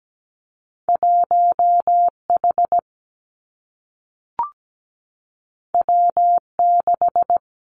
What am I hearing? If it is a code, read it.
Morse code, heard as a keyed tone, about 17 wpm, 700 Hz. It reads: 1H W6
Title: Two Challenging Letter-Number Combinations 17wpm